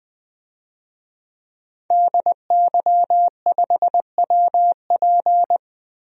Morse code, heard as a keyed tone, about 20 wpm, 700 Hz. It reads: DY5WP
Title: DXCC-100 Call Signs - 5 Characters - Repeated 3x - 20wpm